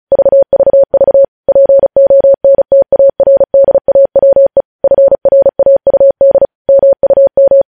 Morse code, heard as a keyed tone, about 36 wpm, 550 Hz. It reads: VVV PONTARDAWE FRAUD MUM
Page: news headlines in morse code at 35 wpm